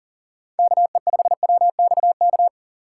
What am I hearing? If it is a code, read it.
Morse code, heard as a keyed tone, about 40 wpm, 700 Hz. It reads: KE5WXK
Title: USA Call Signs - 6 Characters - 2x Character Spacing - 40wpm